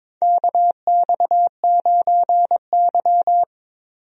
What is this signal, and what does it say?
Morse code, heard as a keyed tone, about 22 wpm, 700 Hz. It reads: KX9Y
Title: USA Call Signs - 4 Characters - 22wpm